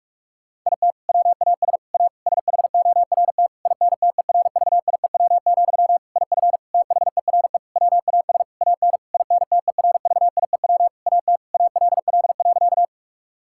Morse code, heard as a keyed tone, about 41 wpm, 700 Hz. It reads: IT WAS A SHORT INTERVIEW, IF THERE WAS AN INTERVIEW AT ALL.
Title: Sentences from Top 1600 Words 45wpm